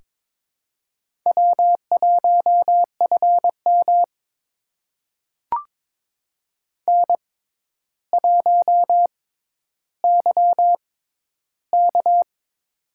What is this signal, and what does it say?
Morse code, heard as a keyed tone, about 21 wpm, spaced as for 11 wpm, 700 Hz. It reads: W1FM N1YK
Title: USA Call Signs - 4 Characters - 6x Character Spacing - 22wpm